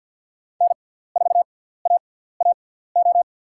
Morse code, heard as a keyed tone, about 47 wpm, spaced as for 17 wpm, 700 Hz. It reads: N4AAO
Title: USA Call Signs - 5 Characters - 6x Character Spacing - 50wpm